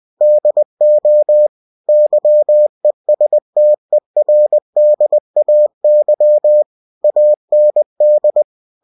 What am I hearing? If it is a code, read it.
Morse code, heard as a keyed tone, about 20 wpm, 600 Hz. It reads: DO YESTERDAY AND
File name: cw-books / winnie-the-pooh / winnie-20wpm-page-0024